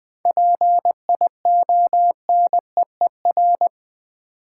Morse code, heard as a keyed tone, about 20 wpm, 700 Hz. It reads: PIONEER
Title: 7 Letter Words - First 3 Letters Given - 20wpm